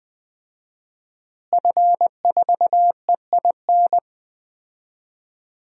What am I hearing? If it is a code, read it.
Morse code, heard as a keyed tone, about 20 wpm, 700 Hz. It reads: F4EIN
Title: DXCC-100 Call Signs - 5 Characters - Repeated 2x - 20wpm